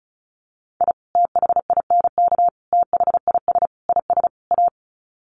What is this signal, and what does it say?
Morse code, heard as a keyed tone, about 34 wpm, 700 Hz. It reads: I THINK THIS IS A
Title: Sets of 5 Words 35wpm